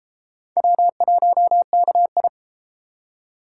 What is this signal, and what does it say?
Morse code, heard as a keyed tone, about 30 wpm, 700 Hz. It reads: W1KI